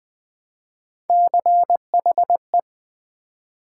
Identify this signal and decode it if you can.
Morse code, heard as a keyed tone, about 20 wpm, 700 Hz. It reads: CHE
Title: Build A Word - 1 Letter At A Time - 5-Letter Words - 20wpm